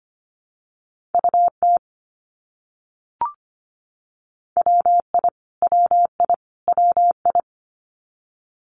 Morse code, heard as a keyed tone, about 25 wpm, 700 Hz. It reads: UT WI WI WI
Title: US States Repeated 3x 25wpm